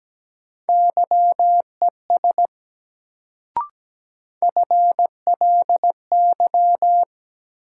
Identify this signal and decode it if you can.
Morse code, heard as a keyed tone, about 17 wpm, 700 Hz. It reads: YES FLY
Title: Sentences from Top 500 Words - New Word Drill - 17wpm